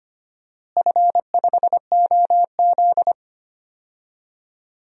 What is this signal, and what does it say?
Morse code, heard as a keyed tone, about 25 wpm, 700 Hz. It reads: F5OZ
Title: DXCC-100 Call Signs - 4 Characters - Repeated 2x - 25wpm